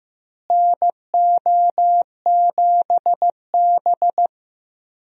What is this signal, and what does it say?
Morse code, heard as a keyed tone, about 15 wpm, 700 Hz. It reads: NO7B
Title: USA Call Signs - 4 Characters - Repeated 2x - 15wpm